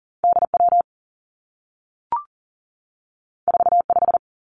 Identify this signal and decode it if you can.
Morse code, heard as a keyed tone, about 38 wpm, 700 Hz. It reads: D W 4 5
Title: Two Challenging Letter-Number Combinations 40wpm